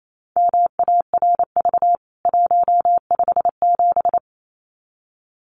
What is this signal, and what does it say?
Morse code, heard as a keyed tone, about 28 wpm, 700 Hz. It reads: MARV 157
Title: CWT Exchange 28wpm